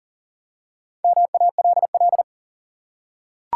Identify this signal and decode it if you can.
Morse code, heard as a keyed tone, about 40 wpm, 700 Hz. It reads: MALL
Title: Four Letter Words - 40wpm